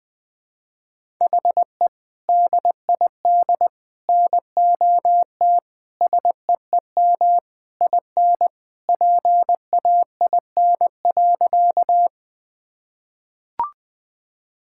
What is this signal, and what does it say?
Morse code, heard as a keyed tone, about 20 wpm, 700 Hz. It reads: HE DID NOT SEEM IN PAIN.
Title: Sentences from Top 1600 Words 20wpm